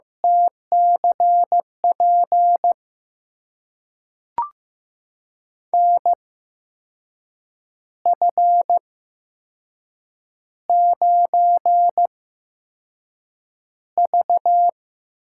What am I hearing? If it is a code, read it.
Morse code, heard as a keyed tone, about 15 wpm, spaced as for 6 wpm, 700 Hz. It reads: TCP NF9V
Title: USA Call Signs - 8x Character Spacing - 15wpm